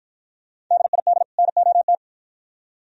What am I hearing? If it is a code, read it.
Morse code, heard as a keyed tone, about 51 wpm, 700 Hz. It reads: DID NOT